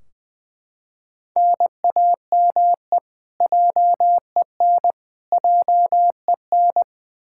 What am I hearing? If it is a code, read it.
Morse code, heard as a keyed tone, about 21 wpm, 700 Hz. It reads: NAME JEN JEN
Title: Practice QSO - Exchange 1 - Beginning - 20wpm